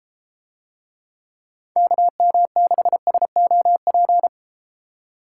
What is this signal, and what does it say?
Morse code, heard as a keyed tone, about 31 wpm, 700 Hz. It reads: KM6SOP